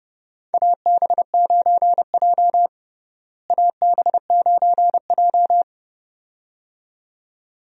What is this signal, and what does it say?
Morse code, heard as a keyed tone, about 30 wpm, 700 Hz. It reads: AB9J AB9J
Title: USA Call Signs - 4 Characters - Repeated 2x - 30wpm